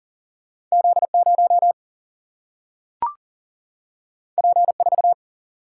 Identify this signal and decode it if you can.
Morse code, heard as a keyed tone, about 42 wpm, 700 Hz. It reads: Z0 P4
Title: Single-Letter Single-Number 40wpm